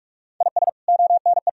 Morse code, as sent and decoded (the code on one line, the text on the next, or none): .. ... / --- -. .
IS ONE